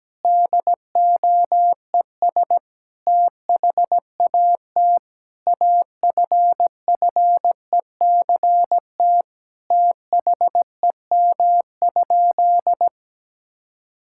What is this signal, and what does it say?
Morse code, heard as a keyed tone, about 17 wpm, 700 Hz. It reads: DOES THAT AFFECT THEM?
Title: Sentences from Top 1000 Words - Review Part 6 - 17wpm